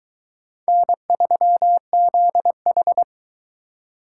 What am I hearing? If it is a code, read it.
Morse code, heard as a keyed tone, about 22 wpm, 700 Hz. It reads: N3ZH